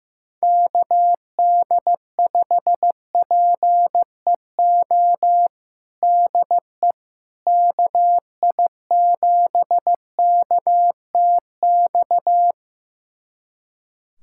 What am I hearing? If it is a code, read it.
Morse code, heard as a keyed tone, about 15 wpm, 700 Hz. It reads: KD5PEO DE KI7KTX
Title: USA Callsign DE Callsign - 6 Characters - 15wpm